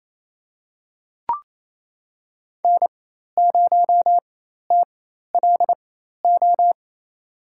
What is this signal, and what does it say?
Morse code, heard as a keyed tone, about 28 wpm, spaced as for 13 wpm, 700 Hz. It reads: N0TLO
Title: USA Call Signs - 5 Characters - 4x Character Spacing - 28wpm